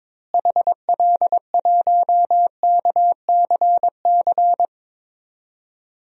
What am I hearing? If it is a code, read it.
Morse code, heard as a keyed tone, about 22 wpm, 700 Hz. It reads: HL1KCC
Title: DXCC-100 Call Signs - 6 Characters - Repeated 3x - 22wpm